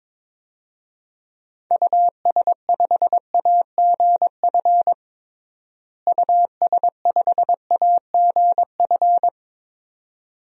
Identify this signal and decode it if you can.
Morse code, heard as a keyed tone, about 22 wpm, 700 Hz. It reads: US5AGF US5AGF
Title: DXCC-100 Call Signs - 6 Characters - Repeated 2x - 22wpm